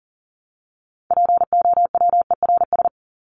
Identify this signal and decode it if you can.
Morse code, heard as a keyed tone, about 40 wpm, 700 Hz. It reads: POWERS
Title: ES:S Suffix - Top 100 Words with Base Words 40wpm